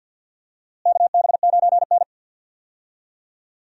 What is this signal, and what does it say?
Morse code, heard as a keyed tone, about 49 wpm, 700 Hz. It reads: KB9N
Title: USA Call Signs - 4 Characters - Repeated 2x - 50wpm